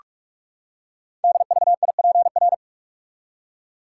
Morse code, heard as a keyed tone, about 44 wpm, 690 Hz. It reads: DVIPR
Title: Code Groups - 5 Letters - 45wpm